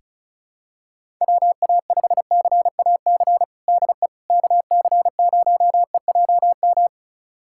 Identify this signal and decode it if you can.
Morse code, heard as a keyed tone, about 36 wpm, 700 Hz. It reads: WA5CAC DE KC0EJM
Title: USA Callsign DE Callsign 35wpm